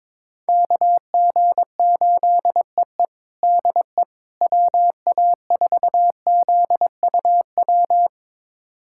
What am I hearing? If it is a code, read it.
Morse code, heard as a keyed tone, about 22 wpm, 700 Hz. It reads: KG8EE DE WA4ZUW
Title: USA Callsign DE Callsign 22wpm